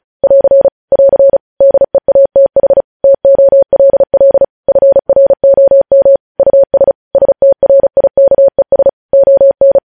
Morse code, heard as a keyed tone, about 36 wpm, 550 Hz. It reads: <AR> <AR> DEATH TOLL FROM US STRIKES ON
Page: News Headlines in Morse Code at 35 WPM